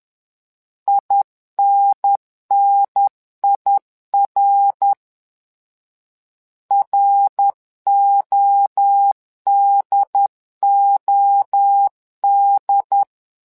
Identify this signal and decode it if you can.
Morse code, heard as a keyed tone, about 11 wpm, 800 Hz. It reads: INNIR RODOD
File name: LEZIONI / 3 MWRO / 3 MWRO 10wpm 2148